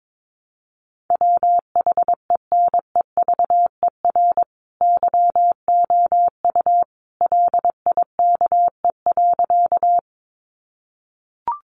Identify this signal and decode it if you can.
Morse code, heard as a keyed tone, about 22 wpm, 700 Hz. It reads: WHENEVER YOU LIKE.
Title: Sentences from Top 1200 Words 22wpm